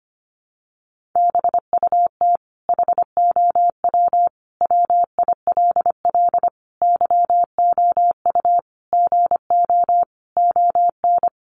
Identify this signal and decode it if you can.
Morse code, heard as a keyed tone, about 25 wpm, 700 Hz. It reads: BUT HOW WILL YOU GO ON
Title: Sentences from Top 1100 Words 25wpm